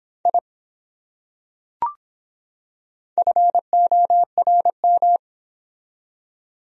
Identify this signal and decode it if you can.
Morse code, heard as a keyed tone, about 22 wpm, 700 Hz. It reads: I FORM